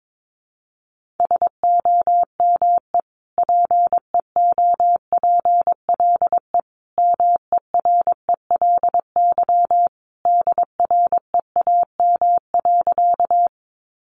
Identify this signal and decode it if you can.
Morse code, heard as a keyed tone, about 22 wpm, 700 Hz. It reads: SOME PEOPLE MERELY DREAM.
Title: Sentences from Top 1500 Words 22wpm